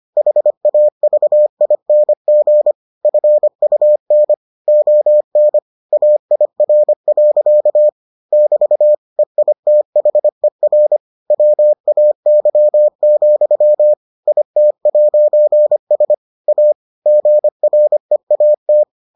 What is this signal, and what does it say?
Morse code, heard as a keyed tone, about 25 wpm, 600 Hz. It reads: HAVING FUN ON AIR. <BT> EITHER WAY, IT'S A GREAT